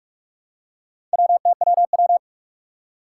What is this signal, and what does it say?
Morse code, heard as a keyed tone, about 44 wpm, 700 Hz. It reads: WTWW
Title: Four Challenging Letter-Number Combinations 45wpm